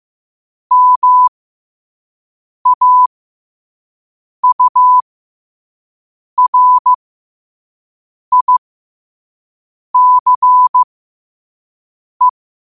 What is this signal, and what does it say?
Morse code, heard as a keyed tone, about 15 wpm, spaced as for 5 wpm, 1000 Hz. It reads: MAURICE
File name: qotdfiles / 5wpm